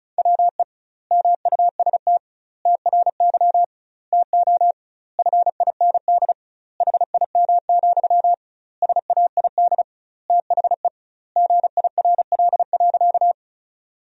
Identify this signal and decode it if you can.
Morse code, heard as a keyed tone, about 34 wpm, 700 Hz. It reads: WE MUST TRY TO FIND HIM, SAID THE GIRL.
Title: 2x Word-Spacing - Sentences from Top 400 Words - 35wpm